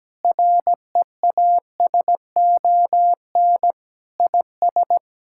Morse code, heard as a keyed tone, about 17 wpm, 700 Hz. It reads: REASON IS
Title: Sentences from Top 1600 Words 17wpm